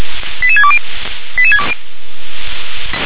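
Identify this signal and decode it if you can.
Five-tone selective call (ZVEI), heard as quick runs of five tones, tones of about 70 ms. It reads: 88620
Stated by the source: ZVEI five-tone call received by SDR.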